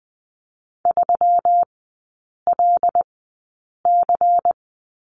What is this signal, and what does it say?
Morse code, heard as a keyed tone, about 19 wpm, spaced as for 8 wpm, 700 Hz. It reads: 3LC
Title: 159 Lesson 26e - Letter g - Three Characters 20wpm